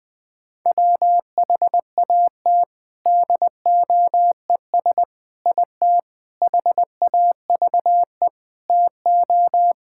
Morse code, heard as a keyed tone, about 20 wpm, 700 Hz. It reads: WHAT DOES IT HAVE TO
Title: Sentences with 4-Word Phrases - 20wpm